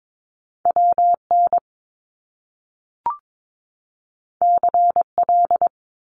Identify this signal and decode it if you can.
Morse code, heard as a keyed tone, about 22 wpm, 700 Hz. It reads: WN CL
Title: Top Two Letter Combinations in Words 22wpm